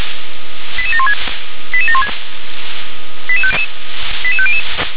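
Five-tone selective call (ZVEI), heard as quick runs of five tones, tones of about 70 ms. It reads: 88716 88716 88500
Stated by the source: ZVEI five-tone selective call received on an SDR.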